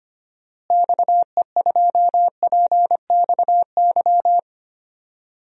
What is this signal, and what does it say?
Morse code, heard as a keyed tone, about 25 wpm, 700 Hz. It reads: XE2PXY
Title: DXCC-100 Call Signs - 6 Characters - Repeated 3x - 25wpm